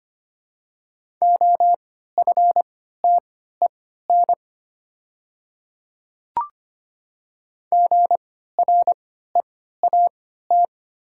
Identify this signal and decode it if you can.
Morse code, heard as a keyed tone, about 25 wpm, spaced as for 14 wpm, 700 Hz. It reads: OFTEN GREAT
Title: 3x Character Spacing - Top 300 Words - 25wpm